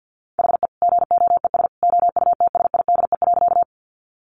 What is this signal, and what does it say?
Morse code, heard as a keyed tone, about 50 wpm, 700 Hz. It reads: HE GOES OUTSIDE.
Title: Additional Easy Phrases 50wpm